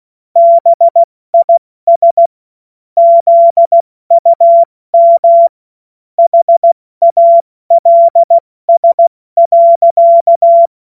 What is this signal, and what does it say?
Morse code, heard as a keyed tone, about 16 wpm, 680 Hz. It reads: BIS ZUM HALS.
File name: Morsekurs / Witze / Witz 14. September 2021, 70 F80